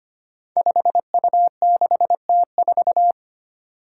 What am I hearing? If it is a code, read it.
Morse code, heard as a keyed tone, about 25 wpm, 700 Hz. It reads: 5U6T4
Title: Five Challenging Letter-Number Combinations 25wpm